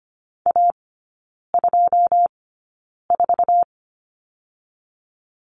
Morse code, heard as a keyed tone, about 25 wpm, spaced as for 9 wpm, 690 Hz. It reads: A24
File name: Lesson 14e - Number 2 - Three Characters 25wpm